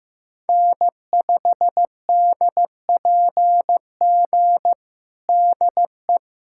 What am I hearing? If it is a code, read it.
Morse code, heard as a keyed tone, about 15 wpm, 700 Hz. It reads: N5DPG DE
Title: USA Callsign DE Callsign - 5 Characters - 15wpm